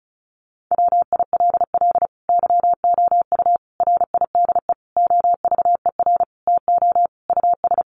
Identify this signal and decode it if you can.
Morse code, heard as a keyed tone, about 34 wpm, 700 Hz. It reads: WILL YOU RIDE OVER TO US
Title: Sentences from Top 1200 Words 35wpm